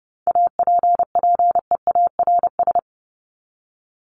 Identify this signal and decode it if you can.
Morse code, heard as a keyed tone, about 30 wpm, 700 Hz. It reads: APPEARS
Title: ES:S Suffix - Top 100 Words with Base Words 30wpm